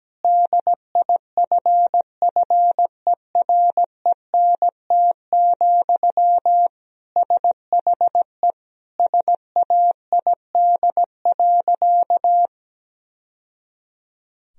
Morse code, heard as a keyed tone, about 17 wpm, 700 Hz. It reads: DIFFERENT, SHE SAID.
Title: Sentences from Top 1000 Words - Review Part 3 - 17wpm